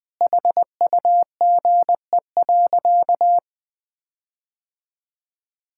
Morse code, heard as a keyed tone, about 20 wpm, 700 Hz. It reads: HUGE.
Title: Sentences with 5-Word Phrases - Spoken Beginning - 20wpm